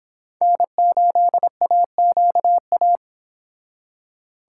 Morse code, heard as a keyed tone, about 26 wpm, 700 Hz. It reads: N8AQA